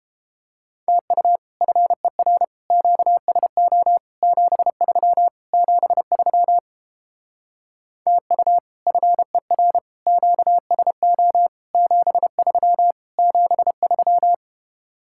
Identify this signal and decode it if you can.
Morse code, heard as a keyed tone, about 31 wpm, 700 Hz. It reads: TU FER QSO 73 73 TU FER QSO 73 73